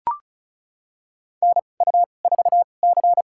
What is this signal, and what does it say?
Morse code, heard as a keyed tone, about 36 wpm, spaced as for 25 wpm, 700 Hz. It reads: NU4C
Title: USA Call Signs - 4 Characters - 2x Character Spacing - 35wpm